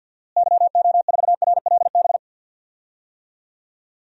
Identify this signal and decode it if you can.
Morse code, heard as a keyed tone, about 51 wpm, 700 Hz. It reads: YO4RLB
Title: DXCC-100 Call Signs - 6 Characters - Repeated 2x - 50wpm